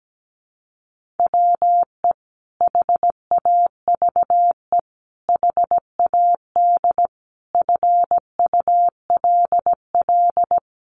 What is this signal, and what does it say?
Morse code, heard as a keyed tone, about 17 wpm, 700 Hz. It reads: WE HAVE HAD FULL
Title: Sentences from Top 1500 Words 17wpm